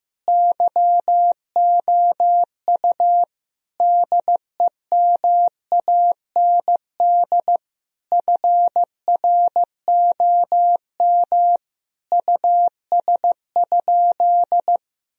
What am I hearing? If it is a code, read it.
Morse code, heard as a keyed tone, about 16 wpm, 700 Hz. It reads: YOU DEMAND FROM US?
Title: Sentences from Top 1000 Words - Review Part 4 - 15wpm